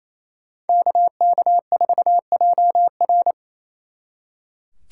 Morse code, heard as a keyed tone, about 36 wpm, 700 Hz. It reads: KK4JR